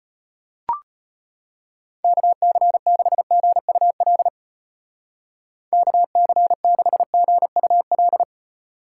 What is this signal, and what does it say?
Morse code, heard as a keyed tone, about 30 wpm, 700 Hz. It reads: KC6GUL KC6GUL
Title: Speed Racing - USA Call Signs 25wpm